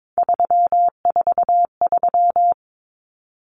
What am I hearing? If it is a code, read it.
Morse code, heard as a keyed tone, about 22 wpm, 700 Hz. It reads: 343
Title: Common Cut Numbers - 3-Digit 22wpm